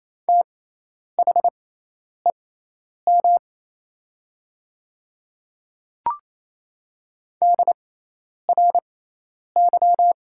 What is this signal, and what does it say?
Morse code, heard as a keyed tone, about 28 wpm, spaced as for 10 wpm, 700 Hz. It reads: THEM DRY
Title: 6x Character Spacing - Top 500 Words - 28wpm